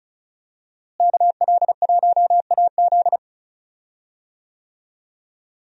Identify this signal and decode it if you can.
Morse code, heard as a keyed tone, about 36 wpm, 700 Hz. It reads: KL1AZ
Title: USA Call Signs - 5 Characters - Repeated 3x - 35wpm